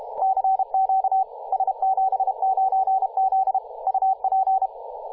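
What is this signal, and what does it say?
Morse code, heard as a keyed tone, about 31 wpm, 770 Hz, with radio noise. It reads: CQ S79Z UP
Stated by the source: Morse from DX station S79Z on 21.002 MHz CW, recorded by JA0RUG.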